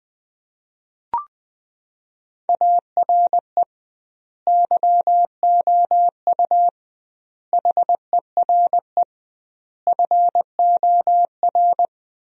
Morse code, heard as a keyed tone, about 21 wpm, 700 Hz. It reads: ARE YOU HERE FOR
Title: 2x Word-Spacing - Sentences from Top 200 Words - 20wpm